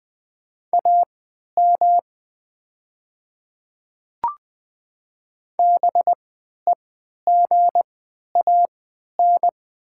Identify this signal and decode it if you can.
Morse code, heard as a keyed tone, about 20 wpm, spaced as for 11 wpm, 700 Hz. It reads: AM BEGAN
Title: 3x Character Spacing - Top 500 Words - 20wpm